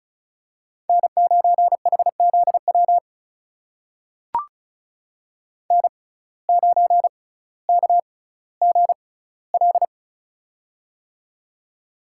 Morse code, heard as a keyed tone, about 36 wpm, spaced as for 18 wpm, 700 Hz. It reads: N9HZW N9KGL
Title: USA Call Signs - 5 Characters - 6x Character Spacing - 35wpm